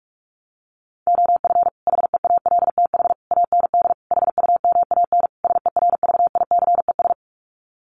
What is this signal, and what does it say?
Morse code, heard as a keyed tone, about 44 wpm, 700 Hz. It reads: OF HEALTH AND HUMAN SERVICES
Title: Sets of 5 Words 45wpm